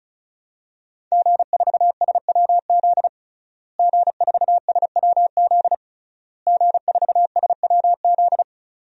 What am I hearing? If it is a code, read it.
Morse code, heard as a keyed tone, about 36 wpm, 700 Hz. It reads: G4SWZ G4SWZ G4SWZ
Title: DXCC-100 Call Signs - 5 Characters - Repeated 3x - 35wpm